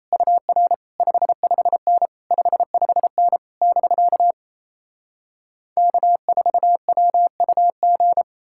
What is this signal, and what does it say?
Morse code, heard as a keyed tone, about 37 wpm, 700 Hz. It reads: UR 55N 55N <BK> K4WUG